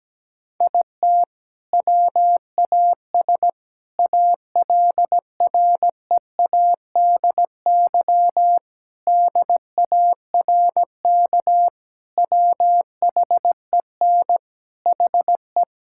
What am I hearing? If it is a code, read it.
Morse code, heard as a keyed tone, about 17 wpm, 700 Hz. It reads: IT WAS ALREADY DARK WHEN HE
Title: Sentences from Top 1000 Words - Review Part 5 - 17wpm